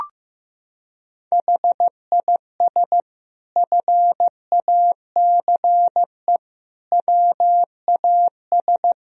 Morse code, heard as a keyed tone, about 15 wpm, 700 Hz. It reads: HIS FACE WAS
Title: Sentences from Top 1000 Words - Review Part 2 - 15wpm